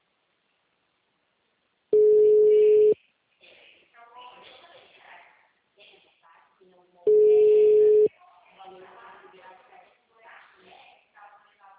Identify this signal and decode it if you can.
Ringback tone (Europe, 425 Hz), with speech.